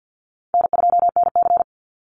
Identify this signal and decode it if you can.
Morse code, heard as a keyed tone, about 49 wpm, 700 Hz. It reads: N2NC
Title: Call Signs Repeated 3x - 50wpm